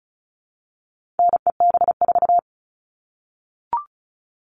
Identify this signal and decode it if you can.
Morse code, heard as a keyed tone, about 34 wpm, 700 Hz. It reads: NEB4
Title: All Cut Numbers - 4-Digit 35wpm